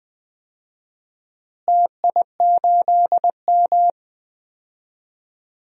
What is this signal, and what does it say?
Morse code, heard as a keyed tone, about 20 wpm, 700 Hz. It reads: TI8M
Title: DXCC-100 Call Signs - 4 Characters - Repeated 3x - 20wpm